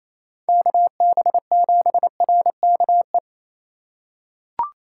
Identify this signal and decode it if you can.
Morse code, heard as a keyed tone, about 28 wpm, 700 Hz. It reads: KB7RKE
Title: USA Call Signs - 6 Characters - 4x Character Spacing - 28wpm